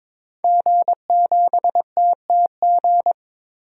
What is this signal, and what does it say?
Morse code, heard as a keyed tone, about 22 wpm, 700 Hz. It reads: G7TTG
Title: DXCC-100 Call Signs - 5 Characters - 22wpm